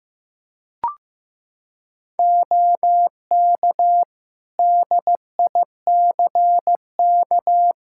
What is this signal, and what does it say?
Morse code, heard as a keyed tone, about 15 wpm, 700 Hz. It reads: OK DICK
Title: Practice QSO - Exchange 2 - Info - 15wpm